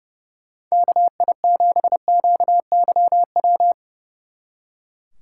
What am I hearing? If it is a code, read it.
Morse code, heard as a keyed tone, about 37 wpm, 700 Hz. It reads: KI7QYW
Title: Speed Racing - USA 2x3 Call Signs 30wpm